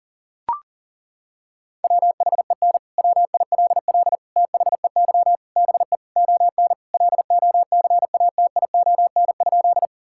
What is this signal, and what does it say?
Morse code, heard as a keyed tone, about 40 wpm, 700 Hz. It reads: WHEN WILL THEY BE ON LOCATION?
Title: Sentences from Top 1000 Words - Review Part 6 - 40wpm